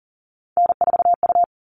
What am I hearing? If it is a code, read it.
Morse code, heard as a keyed tone, about 40 wpm, 700 Hz. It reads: N4U
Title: All Cut Numbers - 3-Digit 40wpm